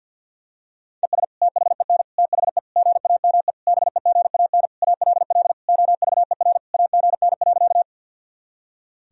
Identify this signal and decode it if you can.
Morse code, heard as a keyed tone, about 51 wpm, 700 Hz. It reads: ES THEN THE GAME BEGAN ALL OVER AGN.
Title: Sentences from Top 500 Words and Abbreviations 50wpm